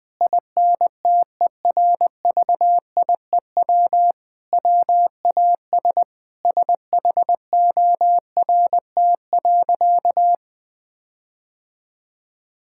Morse code, heard as a keyed tone, about 20 wpm, 700 Hz. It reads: INTERVIEW WAS SHORT.
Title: Sentences from Top 1600 Words 20wpm